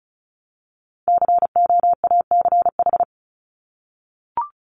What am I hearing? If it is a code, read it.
Morse code, heard as a keyed tone, about 34 wpm, 700 Hz. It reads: COACH
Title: Top 1751-2000 Words 35wpm